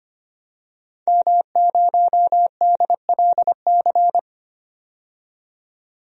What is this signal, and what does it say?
Morse code, heard as a keyed tone, about 25 wpm, 700 Hz. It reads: M0DLC
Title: DXCC-100 Call Signs - 5 Characters - 25wpm